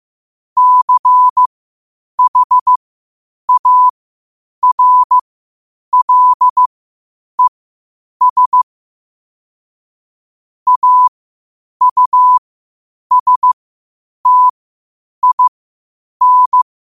Morse code, heard as a keyed tone, about 15 wpm, spaced as for 8 wpm, 1000 Hz. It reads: CHARLES AUSTIN
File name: qotdfiles / 8wpm